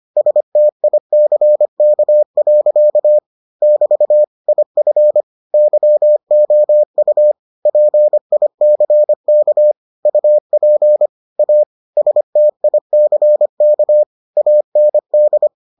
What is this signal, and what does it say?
Morse code, heard as a keyed tone, about 25 wpm, 600 Hz. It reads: STICK. <BT> IF YOU PICK UP A STICK AND